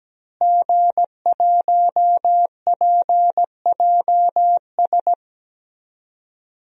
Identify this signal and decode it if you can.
Morse code, heard as a keyed tone, about 17 wpm, 700 Hz. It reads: G1PJS
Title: DXCC-100 Call Signs - 5 Characters - Repeated 3x - 17wpm